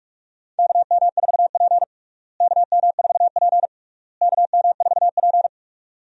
Morse code, heard as a keyed tone, about 41 wpm, 700 Hz. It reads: KM4P KM4P KM4P
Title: USA Call Signs - 4 Characters - Repeated 3x - 45wpm